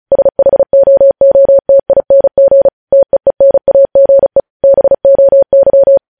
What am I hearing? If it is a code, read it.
Morse code, heard as a keyed tone, about 36 wpm, 550 Hz. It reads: SHOOTING TEENAGE BOY
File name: podcast / News headlines-1x-35x35